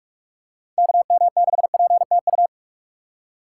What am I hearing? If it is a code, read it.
Morse code, heard as a keyed tone, about 46 wpm, 700 Hz. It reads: KM6PTU
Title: USA Call Signs - 2x Character Spacing - 45wpm